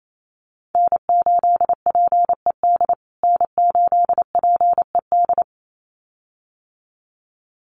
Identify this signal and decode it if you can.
Morse code, heard as a keyed tone, about 28 wpm, 700 Hz. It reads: N8PED N8PED
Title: USA Call Signs Repeated 2x 28wpm